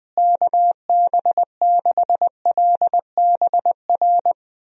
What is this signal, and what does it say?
Morse code, heard as a keyed tone, about 20 wpm, 700 Hz. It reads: KB6LBR
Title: USA Call Signs - 6 Characters - Repeated 2x - 20wpm